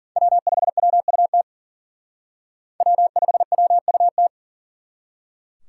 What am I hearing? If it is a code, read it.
Morse code, heard as a keyed tone, about 51 wpm, 700 Hz. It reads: W5WUT W5WUT